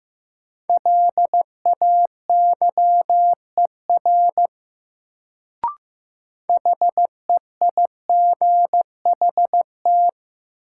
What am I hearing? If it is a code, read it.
Morse code, heard as a keyed tone, about 15 wpm, 700 Hz. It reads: LAYER HEIGHT
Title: Sentences from Top 1600 Words - New Word Drill - 15wpm